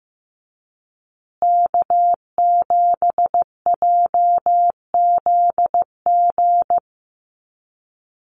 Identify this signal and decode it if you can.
Morse code, heard as a keyed tone, about 15 wpm, 700 Hz. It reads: K7JZG
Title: USA Call Signs Repeated 3x 15wpm